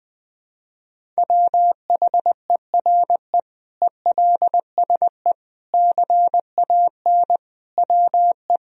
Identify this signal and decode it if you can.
Morse code, heard as a keyed tone, about 20 wpm, 700 Hz. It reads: WHERE ELSE CAN WE
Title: Sentences from Top 1000 Words - Review Part 1 - 20wpm